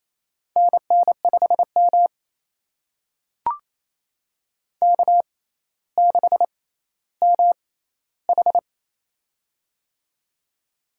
Morse code, heard as a keyed tone, about 27 wpm, spaced as for 14 wpm, 700 Hz. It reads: NN5M K6MH
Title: USA Call Signs - 4 Characters - 6x Character Spacing - 28wpm